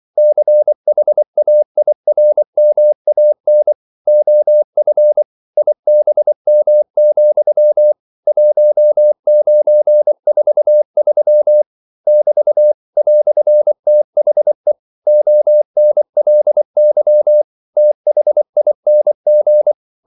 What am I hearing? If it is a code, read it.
Morse code, heard as a keyed tone, about 24 wpm, 600 Hz. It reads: CHAIRMAN OF IBM, 1943 <BT> "THE ONLY THING
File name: CW / Practice / EnglishQuotations0424